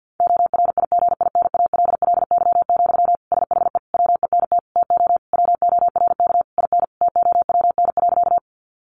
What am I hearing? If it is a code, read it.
Morse code, heard as a keyed tone, about 50 wpm, 700 Hz. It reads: ORIGINALLY, SHE WENT TO WORK IN TOWN.
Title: Sentences from Top 1500 Words 50wpm